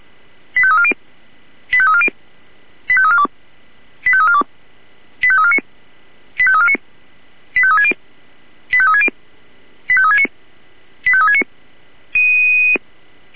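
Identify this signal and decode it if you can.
Five-tone selective call (ZVEI), heard as quick runs of five tones, tones of about 70 ms. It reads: 85340 85340 85342 85342 85349 85349 85360 85360 85369 85369
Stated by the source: ZVEI five-tone selective call 85360, received on SDR.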